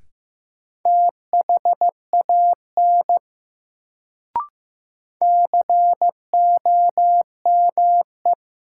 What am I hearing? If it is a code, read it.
Morse code, heard as a keyed tone, about 15 wpm, 700 Hz. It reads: THAN COME
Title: Sets of 2 Words - New Word Drill - 15wpm